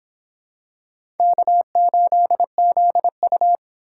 Morse code, heard as a keyed tone, about 26 wpm, 700 Hz. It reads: K8ZU